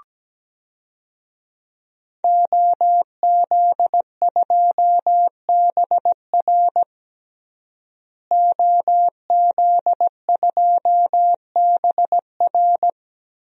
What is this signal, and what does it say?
Morse code, heard as a keyed tone, about 17 wpm, 700 Hz. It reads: OZ2BR OZ2BR
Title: DXCC-100 Call Signs - 5 Characters - Repeated 3x - 17wpm